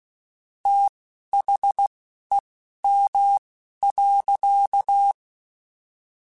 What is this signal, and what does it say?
Morse code, heard as a keyed tone, about 16 wpm, spaced as for 11 wpm, 800 Hz. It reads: THEM.
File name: Chuckles / Chuckles-mp3s / SoftWords @10